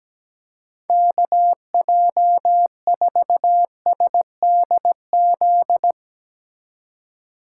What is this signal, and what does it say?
Morse code, heard as a keyed tone, about 19 wpm, 700 Hz. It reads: KJ4SDZ
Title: Speed Racing - USA 2x3 Call Signs 17wpm